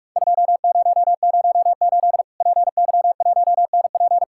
Morse code, heard as a keyed tone, about 41 wpm, 700 Hz. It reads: 2008 PY1NP
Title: Worked All Europe DX Contest - QTC Exchange - 45wpm